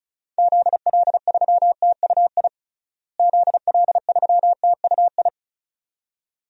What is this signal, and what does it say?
Morse code, heard as a keyed tone, about 36 wpm, 700 Hz. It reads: ZL3TUI ZL3TUI
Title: DXCC-100 Call Signs - 6 Characters - Repeated 2x - 35wpm